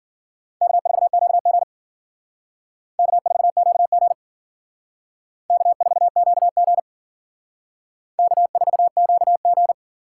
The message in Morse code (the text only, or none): K4QG K4QG K4QG K4QG